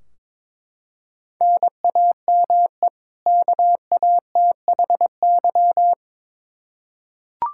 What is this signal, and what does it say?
Morse code, heard as a keyed tone, about 22 wpm, 700 Hz. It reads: NAME KATHY
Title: Practice QSO - Exchange 1 - Beginning - 22wpm